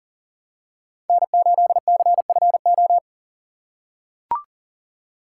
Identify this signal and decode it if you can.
Morse code, heard as a keyed tone, about 37 wpm, 700 Hz. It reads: N8CFO
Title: USA Call Signs - 5 Characters - 4x Character Spacing - 40wpm